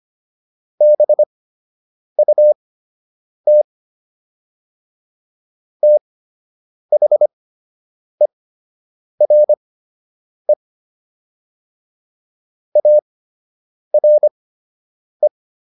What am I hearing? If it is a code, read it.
Morse code, heard as a keyed tone, about 25 wpm, spaced as for 8 wpm, 600 Hz. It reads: BUT THERE ARE